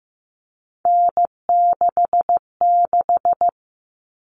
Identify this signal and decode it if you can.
Morse code, heard as a keyed tone, about 15 wpm, 700 Hz. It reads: N66
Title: All Cut Numbers - 3-Digit 15wpm